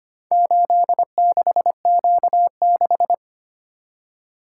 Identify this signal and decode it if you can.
Morse code, heard as a keyed tone, about 25 wpm, 700 Hz. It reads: 86Q6
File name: practice / 25wpm / Four Letter-Number Combinations - 25wpm